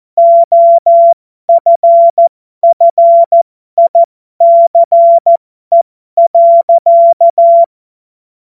Morse code, heard as a keyed tone, about 14 wpm, spaced as for 12 wpm, 680 Hz. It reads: OFFICE.
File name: Morsekurs / Witze